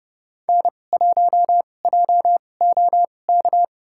Morse code, heard as a keyed tone, about 30 wpm, spaced as for 22 wpm, 700 Hz. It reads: N1JOK
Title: USA Call Signs - 5 Characters - 2x Character Spacing - 30wpm